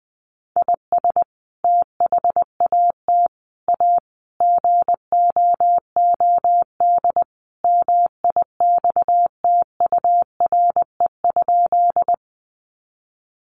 Morse code, heard as a keyed tone, about 20 wpm, 700 Hz. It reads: IS THAT A GOOD MIXTURE?
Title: Sentences from Top 1500 Words 20wpm